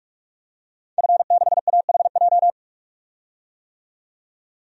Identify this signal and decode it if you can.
Morse code, heard as a keyed tone, about 43 wpm, 700 Hz. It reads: F6AHJ
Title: DXCC-100 Call Signs - 5 Characters - 45wpm